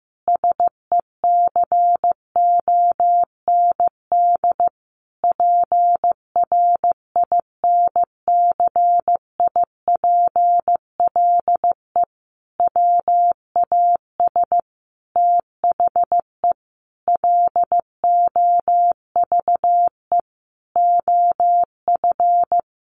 Morse code, heard as a keyed tone, about 15 wpm, 700 Hz. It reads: SECOND PRINCIPLE WAS THE LOVE OF
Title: Sentences from Top 1400 Words 15wpm